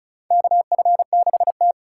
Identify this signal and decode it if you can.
Morse code, heard as a keyed tone, about 36 wpm, 700 Hz. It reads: KF6T
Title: K1USN SST QSO Practice - 35wpm